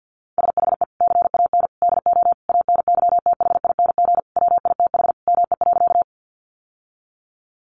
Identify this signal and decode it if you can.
Morse code, heard as a keyed tone, about 50 wpm, 700 Hz. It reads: SHE CAN DO ANYTHING WITH ME.
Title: Sentences from Top 1000 Words 50wpm